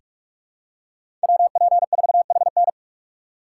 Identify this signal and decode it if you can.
Morse code, heard as a keyed tone, about 46 wpm, 700 Hz. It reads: WP4HN
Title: USA Call Signs - 5 Characters - 6x Character Spacing - 45wpm